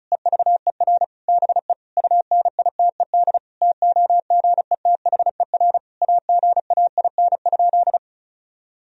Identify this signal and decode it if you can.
Morse code, heard as a keyed tone, about 34 wpm, 700 Hz. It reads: EVER BE UNITED TOGETHER AGAIN?
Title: Sentences from Top 1000 Words - Review Part 4 - 35wpm